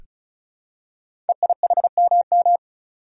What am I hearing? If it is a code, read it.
Morse code, heard as a keyed tone, about 36 wpm, 700 Hz. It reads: EIHMM
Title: Code Groups - 5 Letters - 35wpm